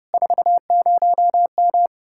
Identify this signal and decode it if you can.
Morse code, heard as a keyed tone, about 36 wpm, 700 Hz. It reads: 40M
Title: Speed-Racing Common QSO Phrases Encore - 30wpm